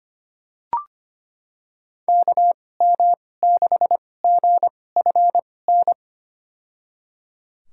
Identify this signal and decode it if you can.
Morse code, heard as a keyed tone, about 25 wpm, spaced as for 18 wpm, 700 Hz. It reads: KM6GFN
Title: USA Call Signs - 6 Characters - 2x Character Spacing - 25wpm